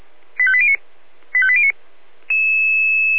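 Five-tone selective call (ZVEI), heard as quick runs of five tones, tones of about 70 ms. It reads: 85809 85809
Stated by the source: ZVEI five-tone call received by SDR.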